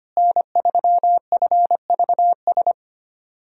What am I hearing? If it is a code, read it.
Morse code, heard as a keyed tone, about 25 wpm, 700 Hz. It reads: N3FVS
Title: USA Call Signs - 5 Characters - Repeated 2x - 25wpm